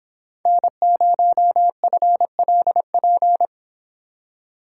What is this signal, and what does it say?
Morse code, heard as a keyed tone, about 26 wpm, 700 Hz. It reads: N0FLP
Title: Speed Racing - USA 1x3 Call Signs 22wpm